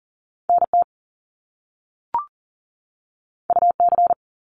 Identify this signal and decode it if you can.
Morse code, heard as a keyed tone, about 38 wpm, 700 Hz. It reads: NT UC
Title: Top Two Letter Combinations in Words 40wpm